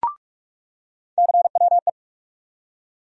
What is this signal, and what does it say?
Morse code, heard as a keyed tone, about 47 wpm, 700 Hz. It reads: CWE
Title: USA Call Sign Postfixes - 3 Letters 45wpm